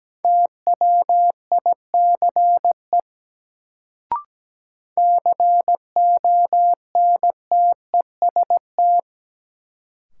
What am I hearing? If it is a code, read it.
Morse code, heard as a keyed tone, about 17 wpm, 700 Hz. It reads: TWICE CONTEST
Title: Sentences from Top 1600 Words - New Word Drill - 17wpm